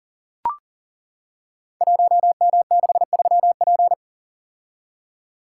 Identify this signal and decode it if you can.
Morse code, heard as a keyed tone, about 38 wpm, 700 Hz. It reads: 1M63P
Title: Code Groups - 5 Letter:Numbers - 40wpm